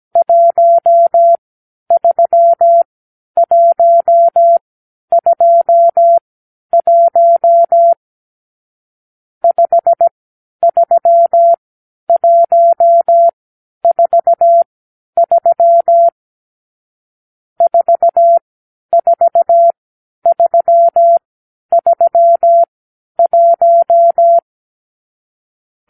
Morse code, heard as a keyed tone, about 17 wpm, spaced as for 10 wpm, 680 Hz. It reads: 13121 53143 44331
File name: k7qo / Lesson 50 (1-6)